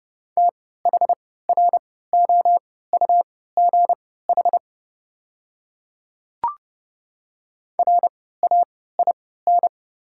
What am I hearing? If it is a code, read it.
Morse code, heard as a keyed tone, about 30 wpm, spaced as for 17 wpm, 700 Hz. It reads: THROUGH RAIN
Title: 3x Character Spacing - Top 500 Words - 30wpm